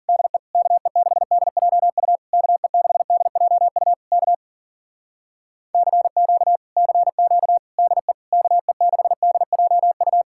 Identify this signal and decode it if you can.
Morse code, heard as a keyed tone, about 52 wpm, 700 Hz. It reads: DE KE6DJU KE6DJU K CQ CQ DE KE6DJU